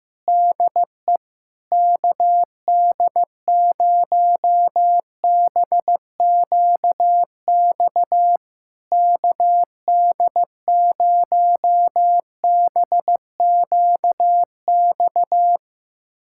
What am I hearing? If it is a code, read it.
Morse code, heard as a keyed tone, about 16 wpm, 700 Hz. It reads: DE KD0BQX KD0BQX
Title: SOTA QSO Practice 15wpm